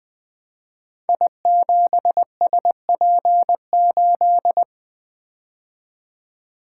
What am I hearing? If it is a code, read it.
Morse code, heard as a keyed tone, about 20 wpm, 700 Hz. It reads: I7SP8
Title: Code Groups - 5 Letter:Numbers - 20wpm